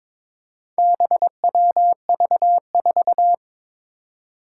Morse code, heard as a keyed tone, about 22 wpm, 700 Hz. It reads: BWV4
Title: Four Challenging Letter-Number Combinations 22wpm